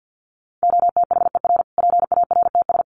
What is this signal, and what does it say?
Morse code, heard as a keyed tone, about 50 wpm, 700 Hz. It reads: OTHER PARTS
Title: Sets of 5 Words Encore 50wpm